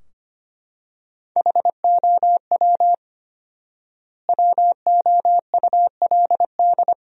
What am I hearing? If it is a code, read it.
Morse code, heard as a keyed tone, about 25 wpm, 700 Hz. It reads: HOW WOULD
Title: 4x Word-Spacing - Sentences from Top 300 Words - 25wpm